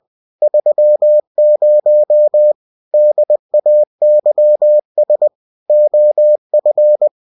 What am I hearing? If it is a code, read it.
Morse code, heard as a keyed tone, about 20 wpm, 600 Hz. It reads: 30 DAYS OF
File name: cw-books / winnie-the-pooh / winnie-20wpm-page-0117